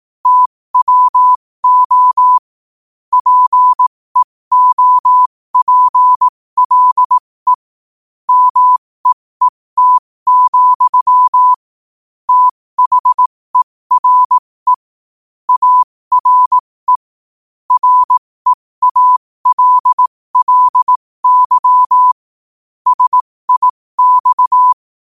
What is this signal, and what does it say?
Morse code, heard as a keyed tone, about 18 wpm, spaced as for 15 wpm, 1000 Hz. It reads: TWO PEOPLE MEET, THERE ARE REALLY SIX
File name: qotdfiles / 15wpm